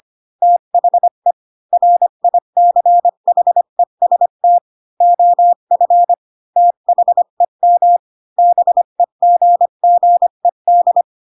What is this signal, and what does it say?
Morse code, heard as a keyed tone, about 25 wpm, 700 Hz. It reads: THE RICHEST OF THEM BEGGED
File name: Pinocchio / Pinocchio ch3 20 25 ik1hge